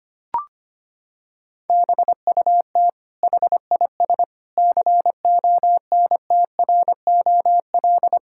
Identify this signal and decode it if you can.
Morse code, heard as a keyed tone, about 25 wpm, 700 Hz. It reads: BUT HIS CONTROL
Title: Sentences from Top 1000 Words - Review Part 6 - 25wpm